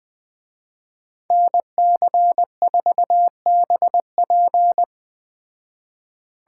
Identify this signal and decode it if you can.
Morse code, heard as a keyed tone, about 20 wpm, 700 Hz. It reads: NC4BP